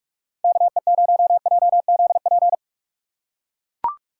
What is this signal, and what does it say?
Morse code, heard as a keyed tone, about 44 wpm, 700 Hz. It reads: KE0JZP